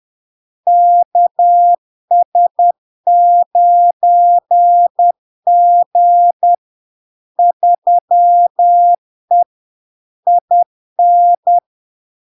Morse code, heard as a keyed tone, about 10 wpm, 700 Hz. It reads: KS9G 3E IN
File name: LARK-FD-2022-Log-in-CW-10wpm-700-Hz